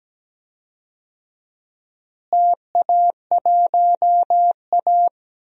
Morse code, heard as a keyed tone, about 17 wpm, 700 Hz. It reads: TA1A
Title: DXCC-100 Call Signs - 4 Characters - Repeated 2x - 17wpm